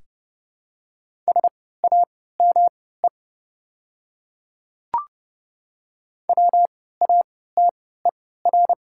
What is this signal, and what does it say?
Morse code, heard as a keyed tone, about 30 wpm, spaced as for 17 wpm, 700 Hz. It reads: SAME WATER
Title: 3x Character Spacing - Top 300 Words - 30wpm